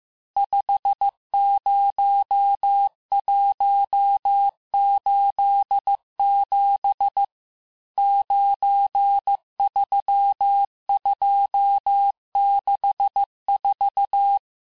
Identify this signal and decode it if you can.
Morse code, heard as a keyed tone, about 15 wpm, 800 Hz. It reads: 50187 93264
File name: cours-du-lundi files / cours-1585-texte-chiffre-vitesse-14mpm